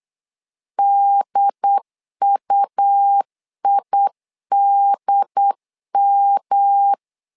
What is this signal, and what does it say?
Morse code, heard as a keyed tone, about 8 wpm, 800 Hz. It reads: DUIDM
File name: LEZIONI / 3 MWRO / 3 M 8wpm 2300